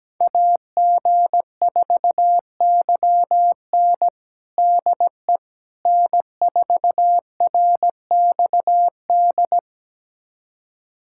Morse code, heard as a keyed tone, about 17 wpm, 700 Hz. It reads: AG4YN DE N4RXD
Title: USA Callsign DE Callsign - 5 Characters - 17wpm